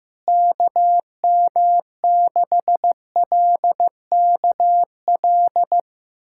Morse code, heard as a keyed tone, about 15 wpm, 700 Hz. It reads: KM6LKL
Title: USA Callsign DE Callsign - 6 Characters - 15wpm